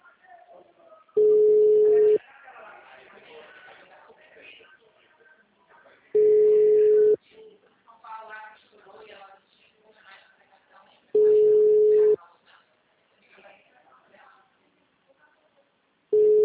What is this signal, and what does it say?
Ringback tone (Europe, 425 Hz), with speech.